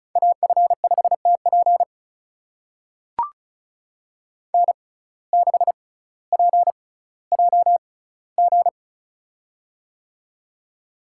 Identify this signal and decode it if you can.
Morse code, heard as a keyed tone, about 36 wpm, spaced as for 18 wpm, 700 Hz. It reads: AF5TP N6PJG
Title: USA Call Signs - 5 Characters - 6x Character Spacing - 35wpm